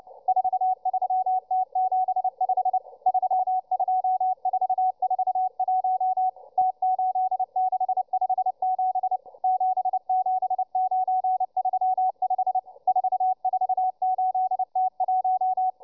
Morse code, heard as a keyed tone, about 29 wpm, 750 Hz, with radio noise. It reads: 43T75 32441 T8657 77935 448T1